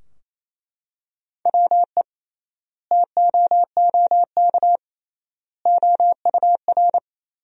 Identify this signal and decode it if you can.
Morse code, heard as a keyed tone, about 28 wpm, 700 Hz. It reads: WE TOOK OUR
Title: 3x Word-Spacing - Sentences from Top 400 Words - 28wpm